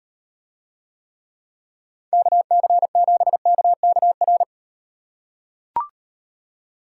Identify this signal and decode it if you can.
Morse code, heard as a keyed tone, about 38 wpm, 700 Hz. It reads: KC7KKR